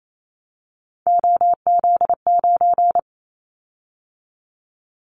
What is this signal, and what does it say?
Morse code, heard as a keyed tone, about 28 wpm, 700 Hz. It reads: OZ9
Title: Callsign Prefixes for Top 100 DXCC - 28wpm